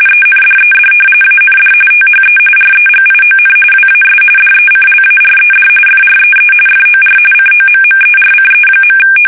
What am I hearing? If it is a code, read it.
RTTY, about 45.45 baud. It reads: ENDED AND THE REGION WAS DECLARED FREE FOR COMMERCE.
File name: fsk850 / usnavy2